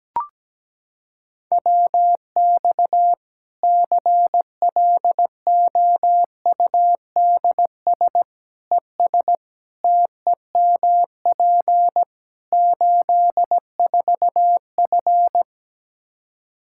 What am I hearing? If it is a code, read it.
Morse code, heard as a keyed tone, about 17 wpm, 700 Hz. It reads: WX CLOUDS ES TEMP 84F
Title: Practice QSO - Exchange 2 - Info - 17wpm